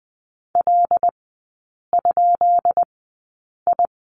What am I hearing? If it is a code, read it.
Morse code, heard as a keyed tone, about 20 wpm, spaced as for 8 wpm, 700 Hz. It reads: L?I
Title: 133 Lesson 22e - Question Mark - Three Characters 20wpm